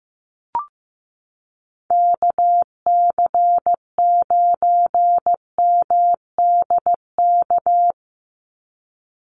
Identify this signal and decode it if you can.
Morse code, heard as a keyed tone, about 15 wpm, 700 Hz. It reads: KC9MDK
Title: USA Call Signs 15wpm